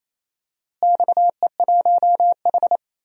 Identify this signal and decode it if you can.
Morse code, heard as a keyed tone, about 28 wpm, 700 Hz. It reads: XE1H
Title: DXCC-100 Call Signs - 4 Characters - Repeated 3x - 28wpm